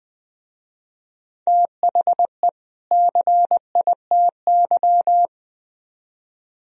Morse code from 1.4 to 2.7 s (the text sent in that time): THE